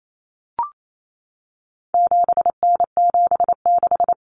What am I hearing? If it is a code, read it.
Morse code, heard as a keyed tone, about 28 wpm, 700 Hz. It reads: 7N76
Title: Common Cut Numbers - 4-Digit 28wpm